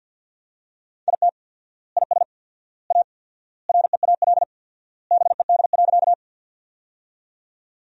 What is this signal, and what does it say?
Morse code, heard as a keyed tone, about 50 wpm, 700 Hz. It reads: IT IS A REAL BED.
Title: 4x Word-Spacing - Sentences from Top 500 Words - 50wpm